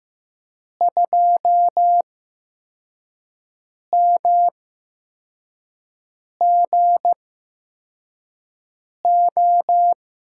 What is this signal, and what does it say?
Morse code, heard as a keyed tone, about 15 wpm, spaced as for 4 wpm, 700 Hz. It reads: 2MGO
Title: USA Call Signs - 5 Characters - 8x Character Spacing - 15wpm